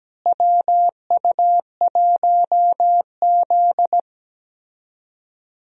Morse code, heard as a keyed tone, about 17 wpm, 700 Hz. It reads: WU1Z
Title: USA Call Signs - 4 Characters - Repeated 3x - 17wpm